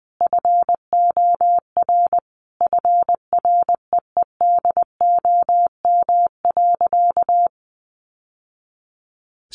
Morse code, heard as a keyed tone, about 20 wpm, 700 Hz. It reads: FOR FREEDOM.
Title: Sentences from Top 800 Words 20wpm